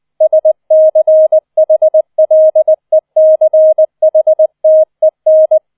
Morse code, heard as a keyed tone, about 20 wpm, 630 Hz. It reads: SCHLECHTEN